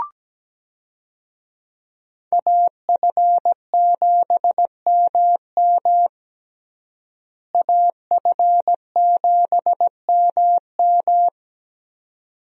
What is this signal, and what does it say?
Morse code, heard as a keyed tone, about 17 wpm, 700 Hz. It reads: AF7MM AF7MM
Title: USA Call Signs - 5 Characters - Repeated 2x - 17wpm